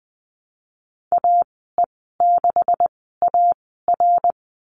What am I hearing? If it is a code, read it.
Morse code, heard as a keyed tone, about 20 wpm, spaced as for 15 wpm, 700 Hz. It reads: AE6AR
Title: 128 Lesson 21g - Number 6 - Call Signs 20wpm